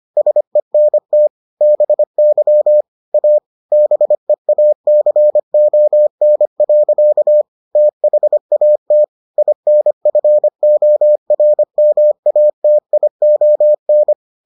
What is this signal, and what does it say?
Morse code, heard as a keyed tone, about 25 wpm, 600 Hz. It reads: SENT BY A BEACON. THAT INFORMATION